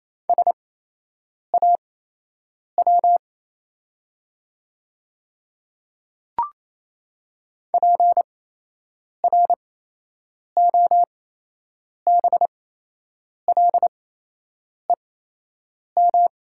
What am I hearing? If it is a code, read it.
Morse code, heard as a keyed tone, about 27 wpm, spaced as for 8 wpm, 700 Hz. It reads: SAW PROBLEM
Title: 8x Character Spacing - Top 500 Words - 28wpm